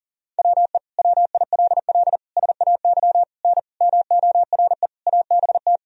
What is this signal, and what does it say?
Morse code, heard as a keyed tone, about 40 wpm, 700 Hz. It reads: WE WILL SAY N MORE ABT
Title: Sentences from Top 200 Words and Abbreviations 40wpm